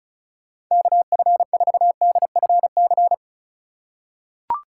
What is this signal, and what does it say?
Morse code, heard as a keyed tone, about 36 wpm, 700 Hz. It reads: KF4DFC
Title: USA Call Signs - 6 Characters - 4x Character Spacing - 35wpm